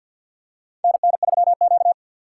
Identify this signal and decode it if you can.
Morse code, heard as a keyed tone, about 52 wpm, 700 Hz. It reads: NN3Q
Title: USA Call Signs - 4 Characters - 3x Character Spacing - 50wpm